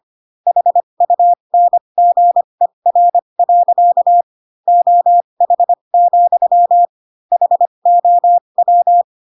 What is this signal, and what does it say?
Morse code, heard as a keyed tone, about 25 wpm, spaced as for 22 wpm, 700 Hz. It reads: HUNGER. OH, HOW